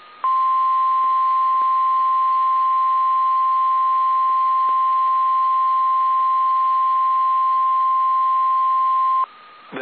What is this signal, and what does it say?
Emergency Alert System (SAME) tones (the 1050 Hz attention tone), with noise. It is heard from 0.2 to 9.3 s.
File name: eas / KSDZKDJL / KSDZKDJL